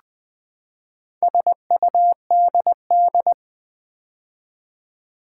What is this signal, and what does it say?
Morse code, heard as a keyed tone, about 20 wpm, 700 Hz. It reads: SUDD
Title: Build A Word - 4 Letters At A Time - 8-Letter Words - 20wpm